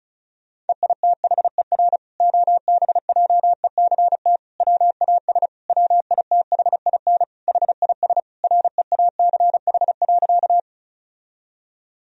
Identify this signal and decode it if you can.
Morse code, heard as a keyed tone, about 34 wpm, 700 Hz. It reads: EITHER OBJECT WAS WITHIN HIS REACH.
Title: Sentences from Top 1000 Words - Review Part 4 - 35wpm